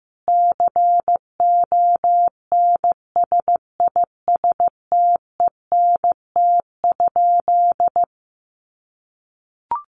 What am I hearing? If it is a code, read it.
Morse code, heard as a keyed tone, about 15 wpm, 700 Hz. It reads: CONSISTENT?
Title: Sentences from Top 1500 Words 15wpm